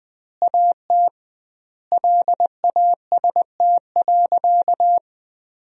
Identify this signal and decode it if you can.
Morse code, heard as a keyed tone, about 20 wpm, 700 Hz. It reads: AT LAST.
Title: 2x Word-Spacing - Sentences from Top 300 Words - 20wpm